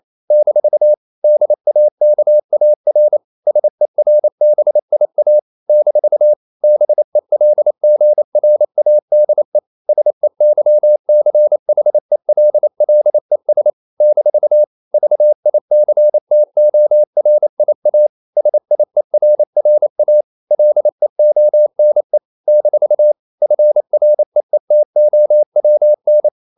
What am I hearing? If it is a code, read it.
Morse code, heard as a keyed tone, about 28 wpm, 600 Hz. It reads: - DAKAR SERBIA - BELGRADE SEYCHELLES - VICTORIA SIERRA LEONE - FREETOWN
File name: CW / Practice / NationalCapitals28